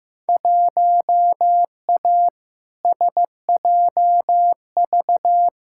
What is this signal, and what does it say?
Morse code, heard as a keyed tone, about 15 wpm, 700 Hz. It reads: 1A SJV
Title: Field Day Exchanges Encore - 15wpm